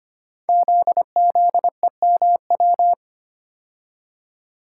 Morse code, heard as a keyed tone, about 25 wpm, 700 Hz. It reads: ZZEMW
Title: Code Groups - 5 Letters - 25wpm